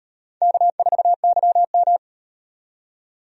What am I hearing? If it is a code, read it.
Morse code, heard as a keyed tone, about 38 wpm, 700 Hz. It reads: K4YM